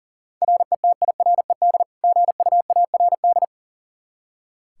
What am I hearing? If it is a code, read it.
Morse code, heard as a keyed tone, about 37 wpm, 700 Hz. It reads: RETIRED GUARD
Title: Practice QSO - Exchange 3 - More Info - 40wpm